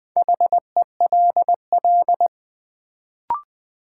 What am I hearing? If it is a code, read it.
Morse code, heard as a keyed tone, about 20 wpm, 700 Hz. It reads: HELL
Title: Sets of 5 Words Encore - New Word Drill - 20wpm